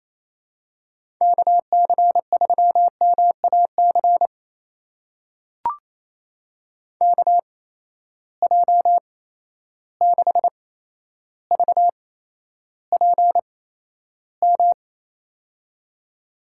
Morse code, heard as a keyed tone, about 28 wpm, spaced as for 12 wpm, 700 Hz. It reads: KC3MAC KJ6VPM
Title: USA Call Signs - 6 Characters - 8x Character Spacing - 28wpm